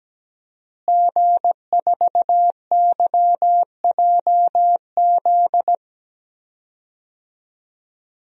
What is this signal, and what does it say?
Morse code, heard as a keyed tone, about 17 wpm, 700 Hz. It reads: G4YJZ
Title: DXCC-100 Call Signs - 5 Characters - Repeated 3x - 17wpm